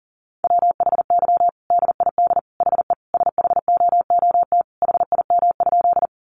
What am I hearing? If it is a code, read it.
Morse code, heard as a keyed tone, about 40 wpm, 700 Hz. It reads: WHY DID HE SHOOT HIM?
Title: Sentences from Top 1200 Words 40wpm